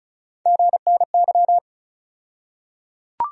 Morse code, heard as a keyed tone, about 35 wpm, 700 Hz. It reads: GNY
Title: USA Call Sign Postfixes - 3 Letters 35wpm